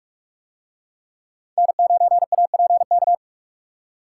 Morse code, heard as a keyed tone, about 44 wpm, 700 Hz. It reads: N9APK